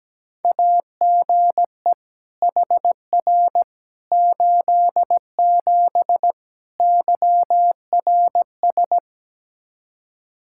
Morse code, heard as a keyed tone, about 17 wpm, 700 Hz. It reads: AGE HR 87 YRS
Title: Practice QSO - Exchange 3 - More Info - 17wpm